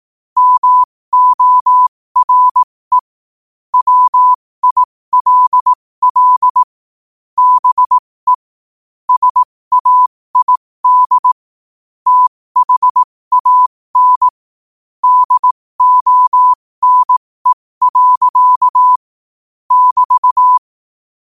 Morse code, heard as a keyed tone, about 18 wpm, spaced as for 15 wpm, 1000 Hz. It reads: MORE WILL BE SAID THAN DONE. <BT>